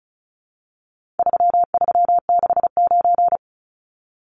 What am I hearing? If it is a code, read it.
Morse code, heard as a keyed tone, about 34 wpm, 700 Hz. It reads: 3369
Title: Numbers - 4-Digit 35wpm